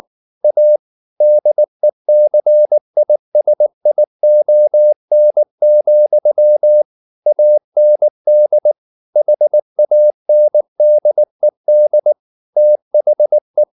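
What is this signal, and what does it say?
Morse code, heard as a keyed tone, about 19 wpm, 600 Hz. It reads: A DECISION, AND HANDED THE
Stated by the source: Morse code at 19 wpm from CW med Gnister, reading Ken Follett's novel Never in English.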